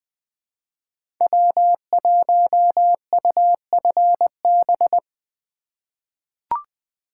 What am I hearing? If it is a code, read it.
Morse code, heard as a keyed tone, about 21 wpm, 700 Hz. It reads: W1UFB
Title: USA Call Signs - 4x Character Spacing - 20wpm